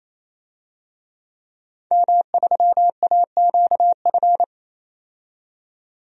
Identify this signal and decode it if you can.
Morse code, heard as a keyed tone, about 28 wpm, 700 Hz. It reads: M3AQF
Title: DXCC-100 Call Signs - 5 Characters - 28wpm